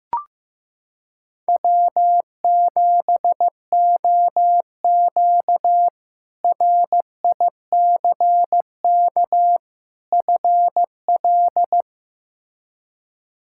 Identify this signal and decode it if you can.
Morse code, heard as a keyed tone, about 15 wpm, 700 Hz. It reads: W7OQ RICK FL
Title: K1USN SST QSO Practice - 15wpm